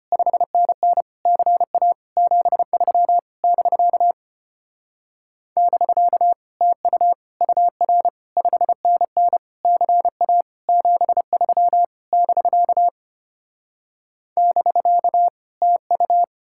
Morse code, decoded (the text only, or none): 5NN CA 73 <BK> <BK> TU UR 5NN CA 73 <BK> <BK> TU